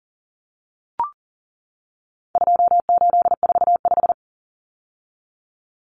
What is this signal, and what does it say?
Morse code, heard as a keyed tone, about 40 wpm, 700 Hz. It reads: 2845
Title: Numbers - 4-Digit 40wpm